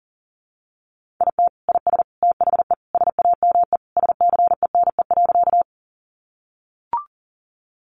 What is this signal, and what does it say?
Morse code, heard as a keyed tone, about 40 wpm, 700 Hz. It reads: IT IS THE SAME SCENE.
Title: Sentences from Top 1100 Words 40wpm